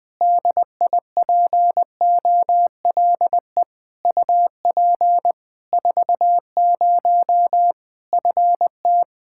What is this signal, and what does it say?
Morse code, heard as a keyed tone, about 20 wpm, 700 Hz. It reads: DIPOLE UP 40 FT
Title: Practice QSO - Exchange 2 - Info - 20wpm